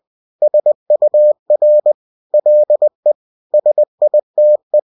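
Morse code, heard as a keyed tone, about 20 wpm, 600 Hz. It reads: SUR LE SITE
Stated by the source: UFT bulletin sent in CW.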